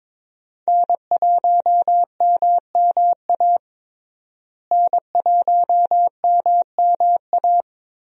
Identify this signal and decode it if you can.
Morse code, heard as a keyed tone, about 22 wpm, 700 Hz. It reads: N1MMA N1MMA
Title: USA Call Signs - 5 Characters - Repeated 2x - 22wpm